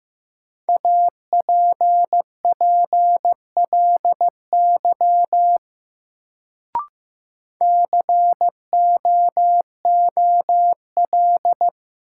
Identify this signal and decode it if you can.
Morse code, heard as a keyed tone, about 15 wpm, 700 Hz. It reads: APPLY COOL
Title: Sentences from Top 1200 Words - New Word Drill - 15wpm